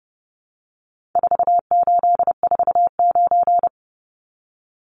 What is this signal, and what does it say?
Morse code, heard as a keyed tone, about 30 wpm, 700 Hz. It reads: 4849
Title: Numbers - 4-Digit 30wpm